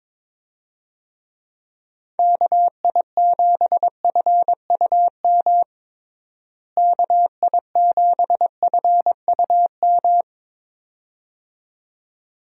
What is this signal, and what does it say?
Morse code, heard as a keyed tone, about 22 wpm, 700 Hz. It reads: KI7FUM KI7FUM
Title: USA Call Signs - 6 Characters - Repeated 2x - 22wpm